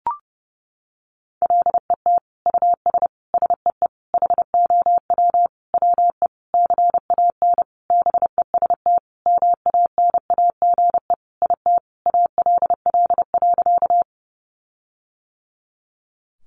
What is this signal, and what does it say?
Morse code, heard as a keyed tone, about 30 wpm, 700 Hz. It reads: LET US SEE HOW WE CAN BEST MANAGE IT ALL.
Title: Sentences from Top 1400 Words 30wpm